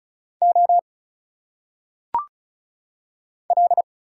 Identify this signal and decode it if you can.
Morse code, heard as a keyed tone, about 36 wpm, 700 Hz. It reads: O L
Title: Single Letters - Phonetic Pronunciation 35wpm